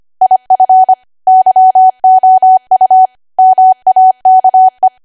Morse code, heard as a keyed tone, about 25 wpm, 750 Hz. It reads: IF YOU MAKE
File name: cwa / W1AW-Level3 / 25WPM